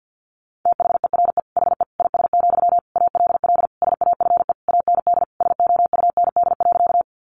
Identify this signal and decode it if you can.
Morse code, heard as a keyed tone, about 48 wpm, 700 Hz. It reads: THERE HE IS, ALL SAFE AND SOUND.
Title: Sentences from Top 900 Words 50wpm